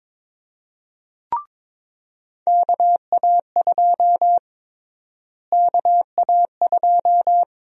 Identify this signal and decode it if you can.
Morse code, heard as a keyed tone, about 22 wpm, 700 Hz. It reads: KA2 KA2
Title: USA Call Sign Prefixes - Repeated 2x - 22wpm